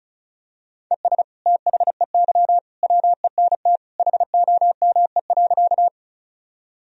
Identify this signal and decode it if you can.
Morse code, heard as a keyed tone, about 34 wpm, 700 Hz. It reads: ES THEY WENT HOME.
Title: Sentences from Top 200 Words and Abbreviations 35wpm